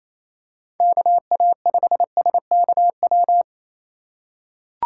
Morse code, heard as a keyed tone, about 27 wpm, 700 Hz. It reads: KA5SKW
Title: USA Call Signs - 6 Characters - 3x Character Spacing - 28wpm